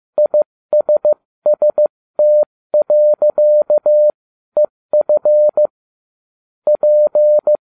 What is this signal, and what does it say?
Morse code, heard as a keyed tone, about 15 wpm, spaced as for 13 wpm, 600 Hz. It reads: ISST.EF P